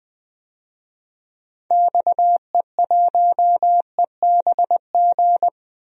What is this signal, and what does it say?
Morse code, heard as a keyed tone, about 20 wpm, 700 Hz. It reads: XE1EBG
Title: DXCC-100 Call Signs - 6 Characters - Repeated 2x - 20wpm